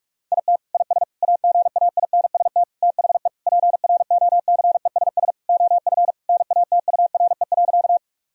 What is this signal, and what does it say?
Morse code, heard as a keyed tone, about 46 wpm, 700 Hz. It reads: IT IS AGAINST THE PROCESS OF NATURE.
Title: Sentences from Top 1000 Words - Review Part 3 - 45wpm